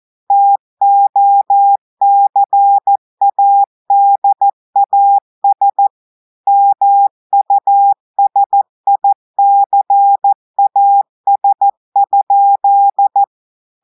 Morse code, heard as a keyed tone, about 14 wpm, 800 Hz. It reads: TOCADAS MUSICAS?
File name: audios / portugais / portugais texte en clair 2 vitesse 14 mots minutes